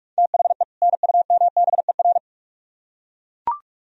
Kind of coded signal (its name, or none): Morse code